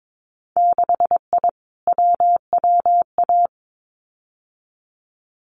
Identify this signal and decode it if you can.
Morse code, heard as a keyed tone, about 22 wpm, 700 Hz. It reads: 6I WWA
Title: Winter Field Day Exchanges Encore 22wpm